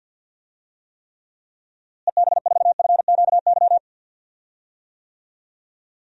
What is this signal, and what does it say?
Morse code, heard as a keyed tone, about 52 wpm, 700 Hz. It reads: EB4FQY
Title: DXCC-100 Call Signs - 6 Characters - Repeated 3x - 50wpm